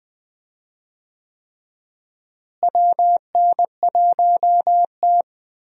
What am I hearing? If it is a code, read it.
Morse code, heard as a keyed tone, about 20 wpm, 700 Hz. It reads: WN1T
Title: USA Call Signs - 4 Characters - Repeated 3x - 20wpm